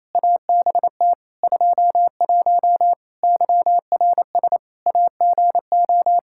Morse code, heard as a keyed tone, about 28 wpm, 700 Hz. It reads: ABT 21 YRS AGO
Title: Practice QSO - Exchange 3 - More Info - 28wpm